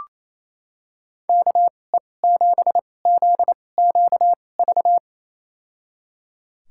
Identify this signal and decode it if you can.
Morse code, heard as a keyed tone, about 28 wpm, spaced as for 20 wpm, 700 Hz. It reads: KE7ZQV
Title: USA Call Signs - 2x Character Spacing - 28wpm